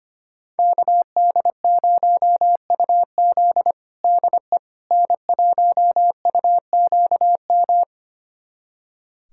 Morse code, heard as a keyed tone, about 25 wpm, 700 Hz. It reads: KD0UZ DE N1UQM
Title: USA Callsign DE Callsign - 5 Characters - 25wpm